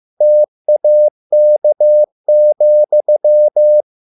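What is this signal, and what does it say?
Morse code, heard as a keyed tone, about 15 wpm, 600 Hz. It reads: TAK,
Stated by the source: Morse code at a slow 15 wpm from CW med Gnister.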